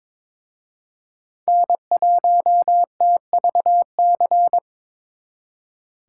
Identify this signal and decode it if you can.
Morse code, heard as a keyed tone, about 26 wpm, 700 Hz. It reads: N1TVC
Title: Speed Racing - USA 1x3 Call Signs 22wpm